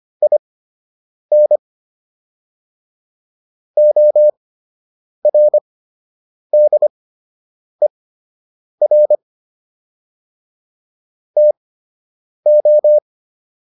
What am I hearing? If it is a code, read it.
Morse code, heard as a keyed tone, about 25 wpm, spaced as for 8 wpm, 600 Hz. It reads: IN ORDER TO